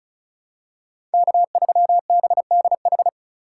Morse code, heard as a keyed tone, about 36 wpm, 700 Hz. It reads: K3BDH
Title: USA Call Signs - 5 Characters - 3x Character Spacing - 35wpm